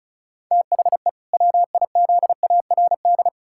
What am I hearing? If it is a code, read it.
Morse code, heard as a keyed tone, about 34 wpm, 700 Hz. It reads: THE WIZARD
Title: Pangram Challenge v2 - 35wpm